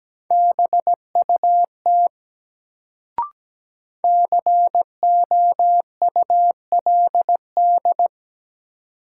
Morse code, heard as a keyed tone, about 17 wpm, 700 Hz. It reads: BUT COULD
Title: Sentences from Top 100 Words - New Word Drill - 17wpm